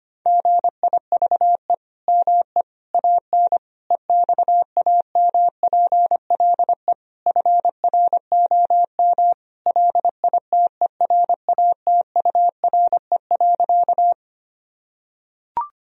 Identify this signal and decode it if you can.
Morse code, heard as a keyed tone, about 25 wpm, 700 Hz. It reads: GIVE ME AN EXAMPLE FROM LITERATURE.
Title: Sentences from Top 1000 Words - Review Part 3 - 25wpm